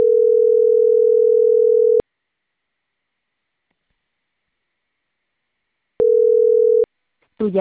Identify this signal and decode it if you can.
Ringback tone (North America), 440 + 480 Hz.